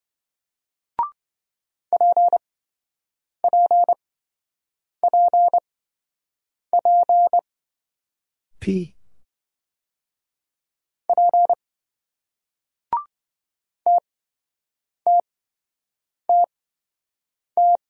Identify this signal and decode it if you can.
Morse code, heard as a keyed tone, about 26 wpm, spaced as for 7 wpm, 700 Hz. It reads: PPPP P TTTT